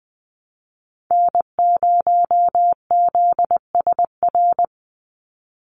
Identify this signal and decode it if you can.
Morse code, heard as a keyed tone, about 20 wpm, 700 Hz. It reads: N0ZSR
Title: USA Call Signs 20wpm